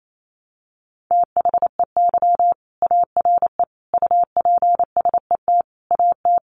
Morse code, heard as a keyed tone, about 28 wpm, 700 Hz. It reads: THEY ARE UPSET AT
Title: Sentences from Top 800 Words 28wpm